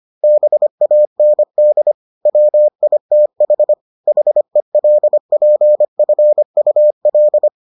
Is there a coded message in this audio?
Yes: Morse code from 0.2 to 7.5 s, heard as a keyed tone, about 25 wpm, 600 Hz. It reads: BAND WITH HELPFUL